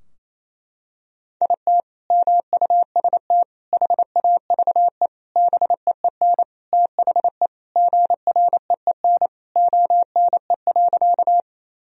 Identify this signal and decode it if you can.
Morse code, heard as a keyed tone, about 28 wpm, 700 Hz. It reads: IT MUST HAVE BEEN THE GREEN ONE.
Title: Sentences from Top 1000 Words - Review Part 2 - 28wpm